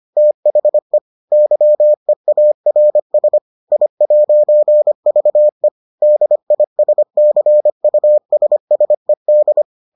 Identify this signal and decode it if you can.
Morse code, heard as a keyed tone, about 25 wpm, 600 Hz. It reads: THE YEARS I'VE DISCUSSED